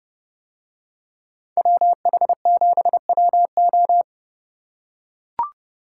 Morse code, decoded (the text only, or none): WH7WO